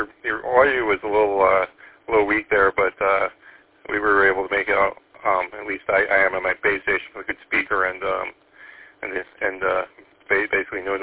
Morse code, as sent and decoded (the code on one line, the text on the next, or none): none